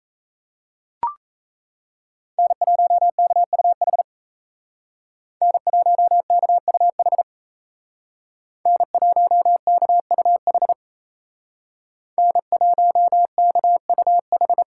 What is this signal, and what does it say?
Morse code, heard as a keyed tone, about 35 wpm, 700 Hz. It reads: N1KUH N1KUH N1KUH N1KUH